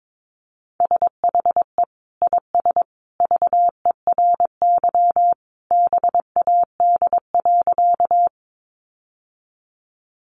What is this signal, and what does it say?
Morse code, heard as a keyed tone, about 22 wpm, 700 Hz. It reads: SHE IS VERY BAD.
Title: Sentences from Top 700 Words 22wpm